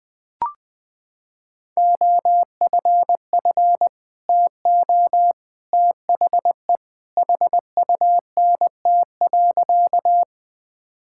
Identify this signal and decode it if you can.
Morse code, heard as a keyed tone, about 20 wpm, 700 Hz. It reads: OFF TO THE HUNT.
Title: Sentences from Top 1600 Words 20wpm